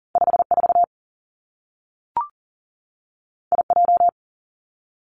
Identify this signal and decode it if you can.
Morse code, heard as a keyed tone, about 42 wpm, 700 Hz. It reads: 5 4 I J
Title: Two Challenging Letter-Number Combinations 40wpm